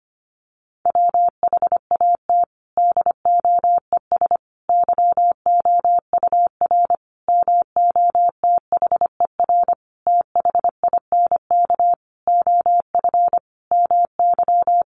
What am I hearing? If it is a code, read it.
Morse code, heard as a keyed tone, about 25 wpm, 700 Hz. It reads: WHAT DOES YOUR MOTHER THINK OF MY
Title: Sentences from Top 1400 Words 25wpm